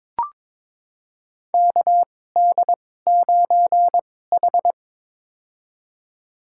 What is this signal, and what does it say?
Morse code, heard as a keyed tone, about 22 wpm, spaced as for 16 wpm, 700 Hz. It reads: KD9H
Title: USA Call Signs - 4 Characters - 2x Character Spacing - 22wpm